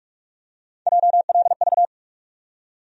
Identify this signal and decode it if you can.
Morse code, heard as a keyed tone, about 45 wpm, 700 Hz. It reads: JLV
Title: USA Call Sign Postfixes - 3 Letters 45wpm